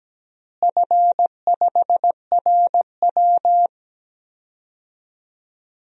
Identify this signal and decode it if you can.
Morse code, heard as a keyed tone, about 17 wpm, 700 Hz. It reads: F5RW
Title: DXCC-100 Call Signs - 4 Characters - Repeated 3x - 17wpm